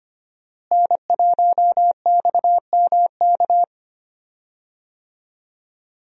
Morse code, heard as a keyed tone, about 25 wpm, 700 Hz. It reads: N1XMK
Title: USA Call Signs - 5 Characters - Repeated 2x - 25wpm